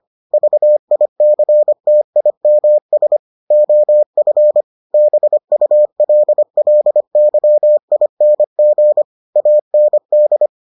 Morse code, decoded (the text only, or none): VICTIMS OF BULLYING AND